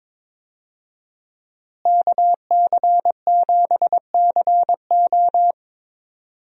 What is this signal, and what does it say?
Morse code, heard as a keyed tone, about 21 wpm, 700 Hz. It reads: KC7CO
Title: USA Call Signs - 5 Characters - Repeated 2x - 22wpm